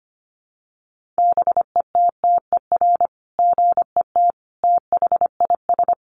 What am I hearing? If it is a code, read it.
Morse code, heard as a keyed tone, about 25 wpm, 700 Hz. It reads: BETTER GET THIS
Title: Sentences from Top 600 Words 25wpm